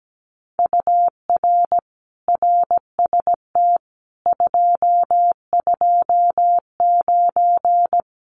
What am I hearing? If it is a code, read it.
Morse code, heard as a keyed tone, about 17 wpm, 700 Hz. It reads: UR RST 229
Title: Common QSO Phrases Encore 17wpm